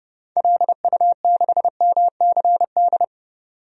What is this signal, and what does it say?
Morse code, heard as a keyed tone, about 30 wpm, 700 Hz. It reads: LU6MCD
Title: DXCC-100 Call Signs - 6 Characters - 30wpm